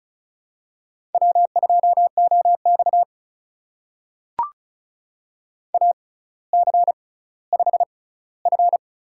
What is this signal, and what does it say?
Morse code, heard as a keyed tone, about 36 wpm, spaced as for 18 wpm, 700 Hz. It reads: W2OX AC5F
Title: USA Call Signs - 4 Characters - 6x Character Spacing - 35wpm